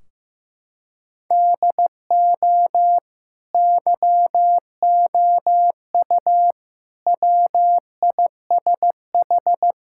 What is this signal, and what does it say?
Morse code, heard as a keyed tone, about 15 wpm, 700 Hz. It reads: DO YOU WISH
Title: Sentences from Top 1600 Words 15wpm